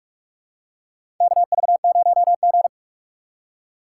Morse code, heard as a keyed tone, about 46 wpm, 700 Hz. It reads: KV0G
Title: USA Call Signs - 4x Character Spacing - 45wpm